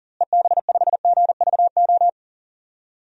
Morse code, heard as a keyed tone, about 37 wpm, 700 Hz. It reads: EB5GVO
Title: DXCC-100 Call Signs - 6 Characters - Repeated 3x - 40wpm